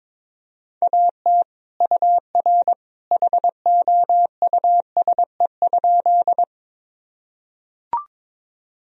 Morse code, heard as a keyed tone, about 22 wpm, 700 Hz. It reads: AT UR HOUSE?
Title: Sentences from Top 200 Words and Abbreviations 22wpm